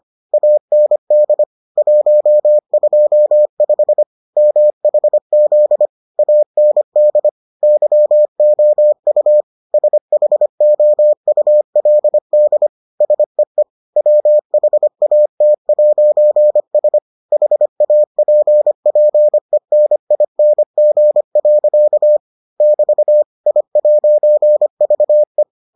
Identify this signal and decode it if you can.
Morse code, heard as a keyed tone, about 25 wpm, 600 Hz. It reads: AND 125 MHZ AND YOU SHOULD SEE WHAT'S HAPPENING. <BT> I'VE